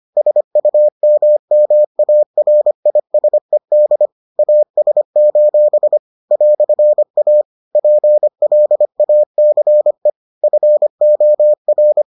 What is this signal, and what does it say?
Morse code, heard as a keyed tone, about 25 wpm, 600 Hz. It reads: SUMMARISED AS: "A PLACE FOR